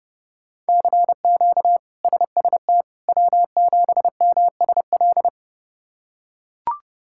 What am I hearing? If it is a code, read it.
Morse code, heard as a keyed tone, about 30 wpm, 700 Hz. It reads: CQ SST W7MSL
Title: K1USN SST QSO Practice - 30wpm